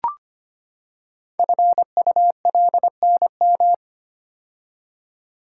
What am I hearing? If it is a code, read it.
Morse code, heard as a keyed tone, about 25 wpm, 700 Hz. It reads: FULNM
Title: Code Groups - 5 Letters - 25wpm